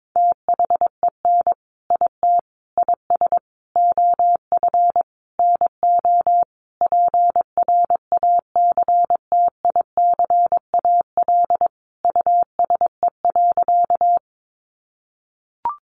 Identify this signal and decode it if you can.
Morse code, heard as a keyed tone, about 22 wpm, 700 Hz. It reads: THEN IT IS OF NO PRACTICAL USE.
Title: Sentences from Top 1400 Words 22wpm